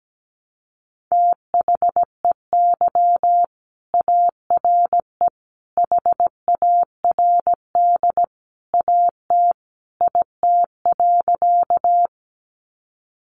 Morse code, heard as a keyed tone, about 17 wpm, 700 Hz. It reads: THEY ARE HARD AT IT.
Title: Sentences from Top 300 Words 17wpm